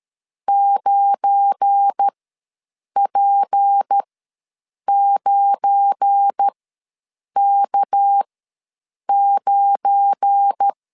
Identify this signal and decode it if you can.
Morse code, heard as a keyed tone, about 13 wpm, spaced as for 7 wpm, 800 Hz. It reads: 9P9K9
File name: LEZIONI / 7 GF9Z / 7 GF9 ws 12wpm 3344